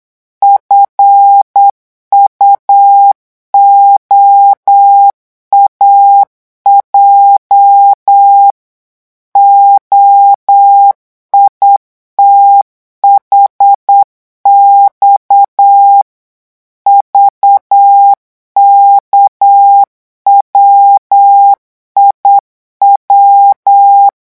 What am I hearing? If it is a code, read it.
Morse code, heard as a keyed tone, about 8 wpm, 800 Hz. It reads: FUOAJ OITHX VKWIW
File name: morse mp3 / revision8-vitesse 8 mots